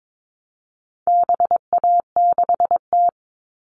Morse code, heard as a keyed tone, about 22 wpm, 700 Hz. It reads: BA6T
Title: All Cut Numbers - 4-Digit 22wpm